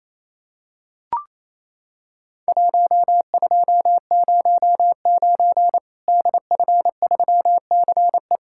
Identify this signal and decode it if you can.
Morse code, heard as a keyed tone, about 28 wpm, 700 Hz. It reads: 1209 DF3CE
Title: Worked All Europe DX Contest - QTC Exchange - 28wpm